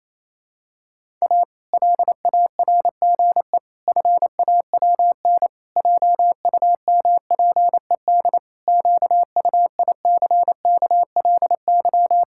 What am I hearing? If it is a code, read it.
Morse code, heard as a keyed tone, about 28 wpm, 700 Hz. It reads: A LARGE FAWN JUMPED QUICKLY
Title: Pangram Challenge v2 - 28wpm